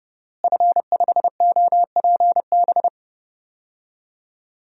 Morse code, heard as a keyed tone, about 30 wpm, 700 Hz. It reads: F5OPB
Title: DXCC-100 Call Signs - 5 Characters - Repeated 2x - 30wpm